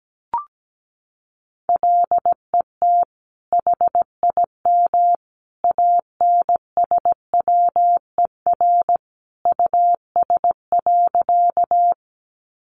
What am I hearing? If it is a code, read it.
Morse code, heard as a keyed tone, about 17 wpm, 700 Hz. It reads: LET HIM ANSWER US.
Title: Sentences from Top 300 Words 17wpm